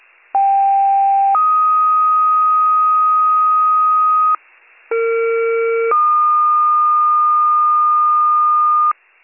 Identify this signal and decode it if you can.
Two-tone page, heard as a one-second tone, then a three-second tone, about 790 then 1250 Hz, about 460 then 1220 Hz. It is heard from 0.3 to 8.9 s.